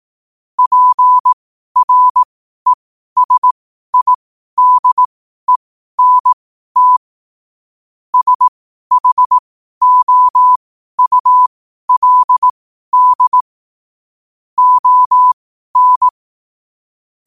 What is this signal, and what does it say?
Morse code, heard as a keyed tone, about 18 wpm, spaced as for 12 wpm, 1000 Hz. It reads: PRESIDENT SHOULD ON
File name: qotdfiles / 12wpm